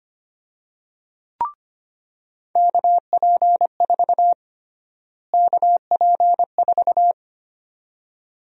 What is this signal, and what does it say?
Morse code, heard as a keyed tone, about 25 wpm, 700 Hz. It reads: KP4 KP4
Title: USA Call Sign Prefixes - Repeated 2x - 25wpm